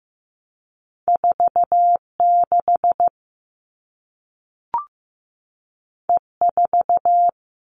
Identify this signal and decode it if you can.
Morse code, heard as a keyed tone, about 15 wpm, 700 Hz. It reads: 46 E4
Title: All Cut Numbers - 2-Digit 15wpm